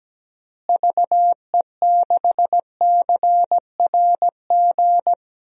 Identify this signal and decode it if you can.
Morse code, heard as a keyed tone, about 17 wpm, 700 Hz. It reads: VE6CRG
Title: DXCC-100 Call Signs - 6 Characters - Repeated 3x - 17wpm